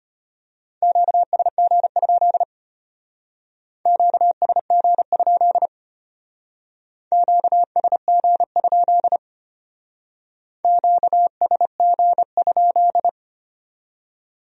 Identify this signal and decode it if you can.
Morse code, heard as a keyed tone, about 33 wpm, 700 Hz. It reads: QSG? QSG? QSG? QSG?